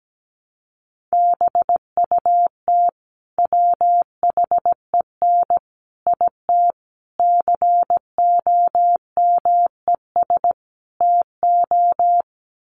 Morse code, heard as a keyed tone, about 17 wpm, 700 Hz. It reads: BUT WHEN IT COMES TO
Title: Sets of 5 Words 17wpm